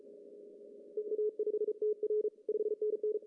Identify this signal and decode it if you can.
Morse code, heard as a keyed tone, about 34 wpm, 400 Hz. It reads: EA5TR 5NN
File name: CW / cqww-cw-k5tr